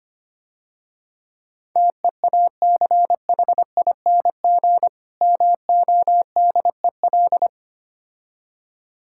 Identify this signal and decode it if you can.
Morse code, heard as a keyed tone, about 26 wpm, 700 Hz. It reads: TEACHING MODEL
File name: practice / 25wpm / Difficult Words - Speed-Racing with Hints - 25wpm